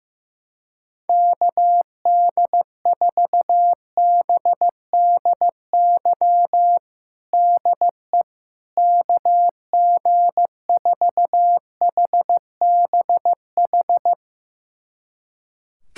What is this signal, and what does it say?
Morse code, heard as a keyed tone, about 15 wpm, 700 Hz. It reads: KD4BDY DE KG4HBH
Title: USA Callsign DE Callsign 15wpm